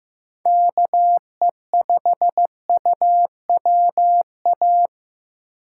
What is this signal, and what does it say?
Morse code, heard as a keyed tone, about 15 wpm, 700 Hz. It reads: KE5UWA
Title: USA Call Signs - 6 Characters - 8x Character Spacing - 15wpm